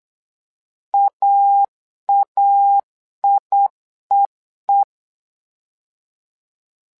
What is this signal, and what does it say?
Morse code, heard as a keyed tone, about 8 wpm, 800 Hz. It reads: AAIEE
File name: LEZIONI / 2 UHDN / 2 UHDN 8wpm2232